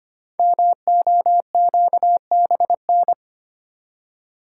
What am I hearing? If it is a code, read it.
Morse code, heard as a keyed tone, about 25 wpm, 700 Hz. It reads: MOQBN
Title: Code Groups - 5 Letters - 25wpm